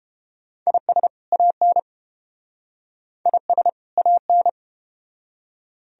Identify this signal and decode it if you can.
Morse code, heard as a keyed tone, about 30 wpm, 700 Hz. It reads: IS AN IS AN